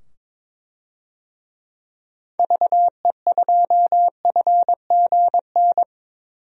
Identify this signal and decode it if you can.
Morse code, heard as a keyed tone, about 22 wpm, 700 Hz. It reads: VE2FGN
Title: DXCC-100 Call Signs - 6 Characters - 22wpm